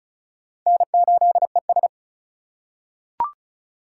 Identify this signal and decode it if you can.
Morse code, heard as a keyed tone, about 34 wpm, 700 Hz. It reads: N8ES
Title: USA Call Signs - 4 Characters - 2x Character Spacing - 35wpm